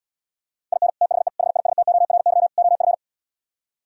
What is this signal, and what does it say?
Morse code, heard as a keyed tone, about 77 wpm, 700 Hz. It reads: IN THE HISTORY OF